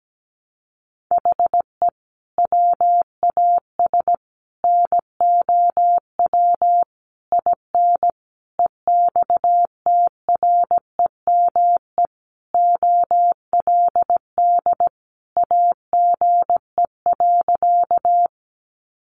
Morse code, heard as a keyed tone, about 17 wpm, 700 Hz. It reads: HE WAS NOW IN EXTREME OLD AGE.
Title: Sentences from Top 1300 Words 17wpm